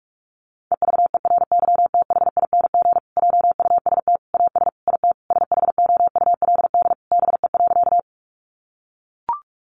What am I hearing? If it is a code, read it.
Morse code, heard as a keyed tone, about 44 wpm, 700 Hz. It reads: EVERYTHING JUST AS IT SHOULD BE.
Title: Sentences from Top 700 Words 45wpm